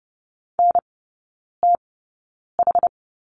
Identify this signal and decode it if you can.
Morse code, heard as a keyed tone, about 30 wpm, spaced as for 9 wpm, 700 Hz. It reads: NTH
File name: Lesson 18e - Letter m - Three Characters 30wpm